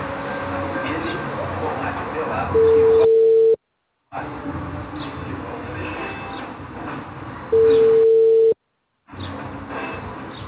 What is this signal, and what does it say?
Ringback tone (China, 450 Hz), with speech.
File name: out- -2008- - - .38